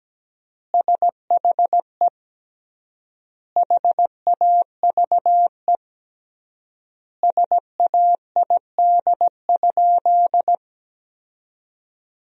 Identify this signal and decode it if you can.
Morse code, heard as a keyed tone, about 17 wpm, 700 Hz. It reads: SHE HAVE SAID?
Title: 3x Word-Spacing - Sentences from Top 100 Words - 17wpm